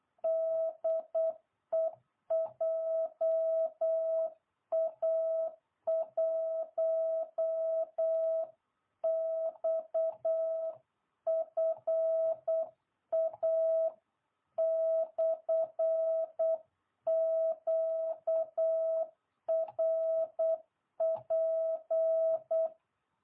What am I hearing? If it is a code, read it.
Morse code, heard as a keyed tone, about 8 wpm, 660 Hz. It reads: DE JA1XFA/QRP